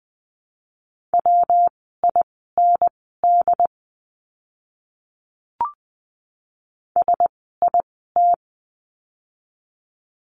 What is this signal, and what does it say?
Morse code, heard as a keyed tone, about 20 wpm, spaced as for 14 wpm, 700 Hz. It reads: WIND SIT
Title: 115 Lesson 19f - Letter w - Words 20wpm